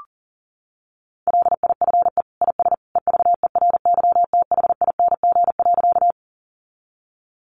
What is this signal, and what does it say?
Morse code, heard as a keyed tone, about 40 wpm, 700 Hz. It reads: LIFE IS EVERYTHING.
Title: Sentences from Top 700 Words 40wpm